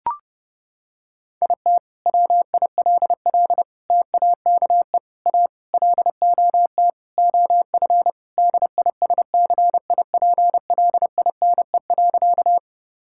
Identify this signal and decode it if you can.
Morse code, heard as a keyed tone, about 30 wpm, 700 Hz. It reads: IT WILL TAKE A LOT OF DISCIPLINE.
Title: Sentences from Top 1000 Words - Review Part 6 - 30wpm